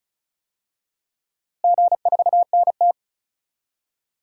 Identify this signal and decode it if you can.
Morse code, heard as a keyed tone, about 36 wpm, 700 Hz. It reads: G4NT
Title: DXCC-100 Call Signs - 4 Characters - Repeated 2x - 35wpm